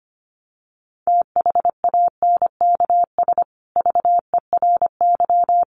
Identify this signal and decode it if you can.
Morse code, heard as a keyed tone, about 25 wpm, 700 Hz. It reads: THANKS VERY
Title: Sets of 5 Words Encore2 25wpm